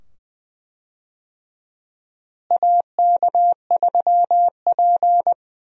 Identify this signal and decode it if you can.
Morse code, heard as a keyed tone, about 20 wpm, 700 Hz. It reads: AK3P
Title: USA Call Signs - 4 Characters - Repeated 2x - 20wpm